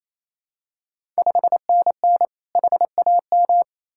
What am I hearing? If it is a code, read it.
Morse code, heard as a keyed tone, about 28 wpm, 690 Hz. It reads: 5NN HAM
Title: Ontario QSO Party Exchange 28wpm